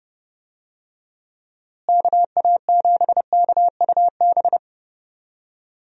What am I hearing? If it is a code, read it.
Morse code, heard as a keyed tone, about 30 wpm, 700 Hz. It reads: KA7KUB
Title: USA Call Signs - 6 Characters - Repeated 2x - 30wpm